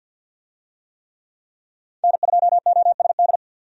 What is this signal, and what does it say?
Morse code, heard as a keyed tone, about 50 wpm, 700 Hz. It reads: N2OSD
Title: USA Call Signs - 5 Characters - 50wpm